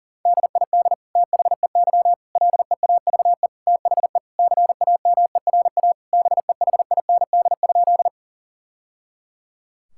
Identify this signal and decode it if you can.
Morse code, heard as a keyed tone, about 40 wpm, 700 Hz. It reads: DID THEY LEAVE THE CAMERA BEHIND?
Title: Sentences from Top 1000 Words - Review Part 5 - 40wpm